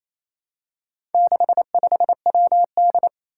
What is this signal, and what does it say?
Morse code, heard as a keyed tone, about 27 wpm, 700 Hz. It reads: 65WD
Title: Four Challenging Letter-Number Combinations 28wpm